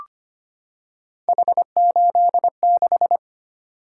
Morse code, heard as a keyed tone, about 25 wpm, 700 Hz. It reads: H86
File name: Three Letter-Number Combinations - 25wpm